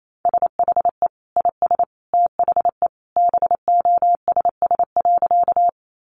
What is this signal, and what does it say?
Morse code, heard as a keyed tone, about 28 wpm, 700 Hz. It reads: SHE IS THE BOSS.
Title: Sentences from Top 700 Words 28wpm